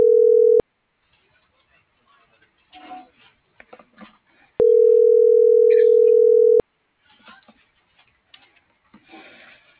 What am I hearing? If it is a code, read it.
Ringback tone (North America), 440 + 480 Hz.